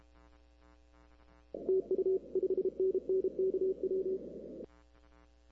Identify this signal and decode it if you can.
Morse code, heard as a keyed tone, about 29 wpm, 370 Hz. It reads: TU 5NN KW
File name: arrl cw dx